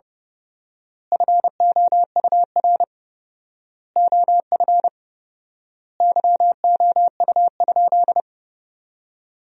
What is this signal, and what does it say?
Morse code, heard as a keyed tone, about 32 wpm, 700 Hz. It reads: FOUR OF YOU?
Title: 4x Word-Spacing - Sentences from Top 300 Words - 30wpm